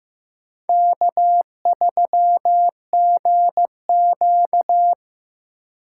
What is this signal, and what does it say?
Morse code, heard as a keyed tone, about 15 wpm, 700 Hz. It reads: K3GQ
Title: USA Call Signs - 4 Characters - 8x Character Spacing - 15wpm